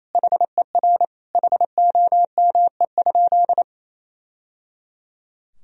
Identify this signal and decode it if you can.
Morse code, heard as a keyed tone, about 28 wpm, 700 Hz. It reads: HER HOME?
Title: Sentences from Top 1000 Words - Review Part 4 - 28wpm